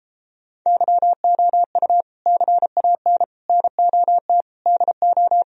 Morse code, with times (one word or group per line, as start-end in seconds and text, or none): 0.7-2.0 YOU
2.3-3.2 CAN
3.5-4.4 NOT
4.7-5.4 DO